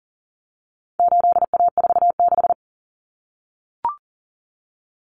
Morse code, heard as a keyed tone, about 40 wpm, 700 Hz. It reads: 8A46
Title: Common Cut Numbers - 4-Digit 40wpm